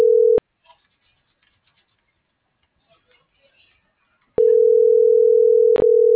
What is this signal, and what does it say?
Ringback tone (North America), 440 + 480 Hz.
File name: CENTRAL MAYAB / out fernanda.pamela